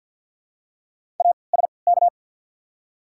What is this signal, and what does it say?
Morse code, heard as a keyed tone, about 54 wpm, 690 Hz. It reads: A S K
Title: 3x Character Spacing - Top 300 Words - 50wpm